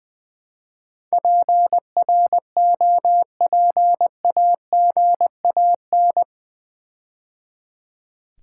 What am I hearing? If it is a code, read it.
Morse code, heard as a keyed tone, about 20 wpm, 700 Hz. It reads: PROPAGAN
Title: Build A Word - 4 Letters At A Time - 10-Letter Words - 20wpm